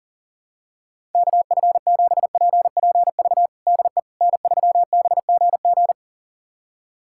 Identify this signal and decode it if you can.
Morse code, heard as a keyed tone, about 40 wpm, 700 Hz. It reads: KF7PPV DE N3BGG
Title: USA Callsign DE Callsign 40wpm